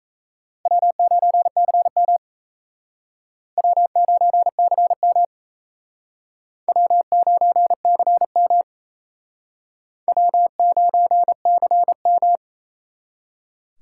Morse code, heard as a keyed tone, about 35 wpm, 700 Hz. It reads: W9CM W9CM W9CM W9CM